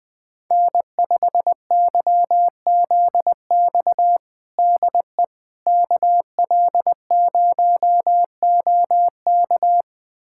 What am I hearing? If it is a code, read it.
Morse code, heard as a keyed tone, about 20 wpm, 700 Hz. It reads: N5YZX DE KL0OK
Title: USA Callsign DE Callsign - 5 Characters - 20wpm